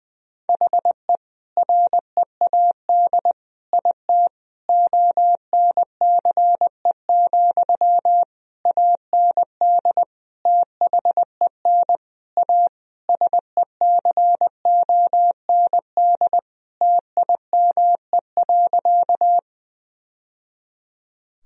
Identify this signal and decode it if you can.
Morse code, heard as a keyed tone, about 19 wpm, 700 Hz. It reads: HE READ IT ONCE, AND THEN A SECOND TIME.
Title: Sentences from Top 1000 Words - Review Part 1 - 20wpm